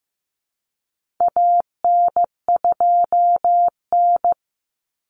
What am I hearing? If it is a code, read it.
Morse code, heard as a keyed tone, about 15 wpm, 700 Hz. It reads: AN2N
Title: Common Cut Numbers - 4-Digit 15wpm